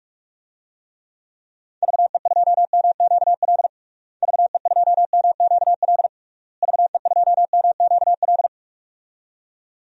Morse code, heard as a keyed tone, about 43 wpm, 700 Hz. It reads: VE2MQL VE2MQL VE2MQL
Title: DXCC-100 Call Signs - 6 Characters - Repeated 3x - 45wpm